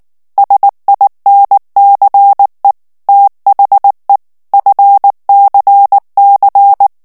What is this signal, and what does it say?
Morse code, heard as a keyed tone, about 19 wpm, 800 Hz. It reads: SINCE THE FCC
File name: ANGLAIS texte clair 3 vitesse 18 mots minute